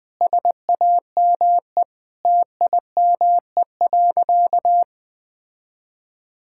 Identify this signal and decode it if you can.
Morse code, heard as a keyed tone, about 20 wpm, 700 Hz. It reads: SAME TIME.
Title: Sentences with 3-Word Phrases - 20wpm